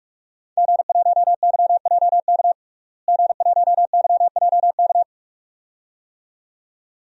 Morse code, heard as a keyed tone, about 41 wpm, 700 Hz. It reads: G1YJK G1YJK
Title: DXCC-100 Call Signs - 5 Characters - Repeated 2x - 45wpm